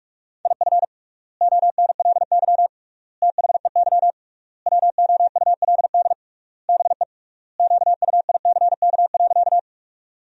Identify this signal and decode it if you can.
Morse code, heard as a keyed tone, about 46 wpm, 700 Hz. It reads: IF ONLY THEY WOULD BE QUICK.
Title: 3x Word-Spacing - Sentences from Top 500 Words - 45wpm